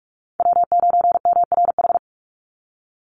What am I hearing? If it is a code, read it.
Morse code, heard as a keyed tone, about 44 wpm, 700 Hz. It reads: W9MRH
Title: NA SOTA Chaser Callsigns 45wpm